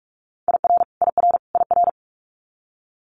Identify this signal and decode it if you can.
Morse code, heard as a keyed tone, about 46 wpm, 700 Hz. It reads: IR IR IR
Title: Multi-letter Phonemes Repeated 3x 45wpm